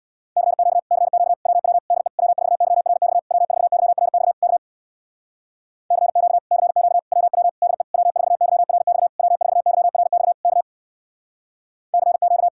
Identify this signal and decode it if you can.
Morse code, heard as a keyed tone, about 67 wpm, 700 Hz. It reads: CQ CQ WW DE W4YRX W4YRX K CQ CQ WW DE W4YRX W4YRX K CQ